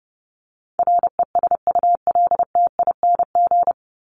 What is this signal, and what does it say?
Morse code, heard as a keyed tone, about 30 wpm, 700 Hz. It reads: RESULTING
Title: ING Suffix - Top 100 Words with Base Words 30wpm